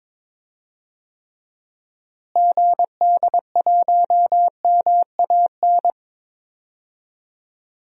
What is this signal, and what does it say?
Morse code, heard as a keyed tone, about 21 wpm, 700 Hz. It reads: GD1MAN
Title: DXCC-100 Call Signs - 6 Characters - Repeated 3x - 22wpm